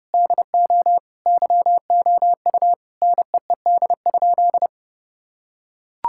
Morse code, heard as a keyed tone, about 30 wpm, 700 Hz. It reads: DO YOU NEED?
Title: Sentences from Top 1000 Words - Review Part 5 - 30wpm